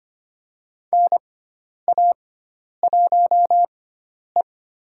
Morse code, heard as a keyed tone, about 25 wpm, spaced as for 10 wpm, 700 Hz. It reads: NA1E
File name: Lesson 16g - Letter c - Call Signs 25wpm